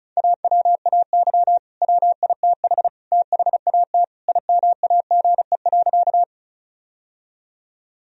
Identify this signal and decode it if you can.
Morse code, heard as a keyed tone, about 36 wpm, 700 Hz. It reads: AWAY WITH THAT IMAGE.
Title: Sentences from Top 1000 Words - Review Part 4 - 35wpm